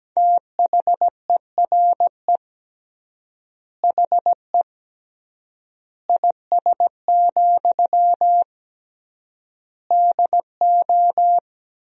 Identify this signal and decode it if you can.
Morse code, heard as a keyed tone, about 17 wpm, 700 Hz. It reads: THERE HE IS, DO
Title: 3x Word-Spacing - Sentences from Top 100 Words - 17wpm